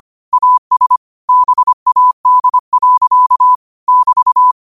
Morse code, heard as a keyed tone, about 25 wpm, 1000 Hz. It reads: AS BAD. <BT>
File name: qotdfiles / 25wpm